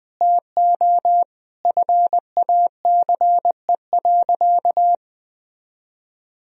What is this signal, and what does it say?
Morse code, heard as a keyed tone, about 20 wpm, 700 Hz. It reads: TO FACE.
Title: Sentences from Top 1000 Words - Review Part 2 - 20wpm